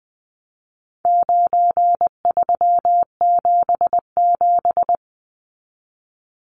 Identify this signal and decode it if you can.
Morse code, heard as a keyed tone, about 20 wpm, 700 Hz. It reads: 9377
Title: Numbers - 4-Digit 20wpm